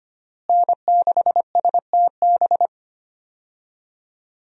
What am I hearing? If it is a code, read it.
Morse code, heard as a keyed tone, about 25 wpm, 700 Hz. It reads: N6STB
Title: USA Call Signs - 5 Characters - Repeated 2x - 25wpm